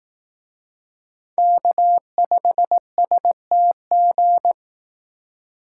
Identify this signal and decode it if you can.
Morse code, heard as a keyed tone, about 18 wpm, 700 Hz. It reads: K5STG
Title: Speed Racing - USA 1x3 Call Signs 15wpm